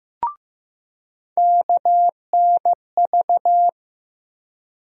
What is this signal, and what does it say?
Morse code, heard as a keyed tone, about 15 wpm, 700 Hz. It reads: KNV
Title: USA Call Sign Postfixes - 3 Letters 15wpm